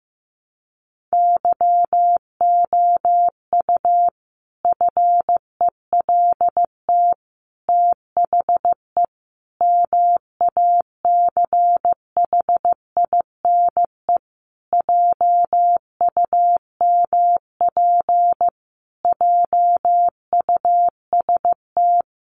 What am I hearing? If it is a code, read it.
Morse code, heard as a keyed tone, about 15 wpm, 700 Hz. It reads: YOU FELT THE MACHINE JUMP JUST
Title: A Sound Of Thunder 15wpm